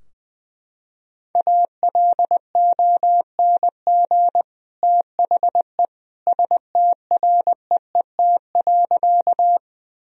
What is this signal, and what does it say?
Morse code, heard as a keyed tone, about 21 wpm, 700 Hz. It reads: ALONG THE STREET.
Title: Sentences from Top 1000 Words - Review Part 3 - 20wpm